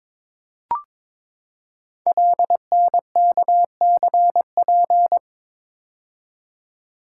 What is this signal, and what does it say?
Morse code, heard as a keyed tone, about 22 wpm, 700 Hz. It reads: LNKCP
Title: Code Groups - 5 Letter:Numbers - 22wpm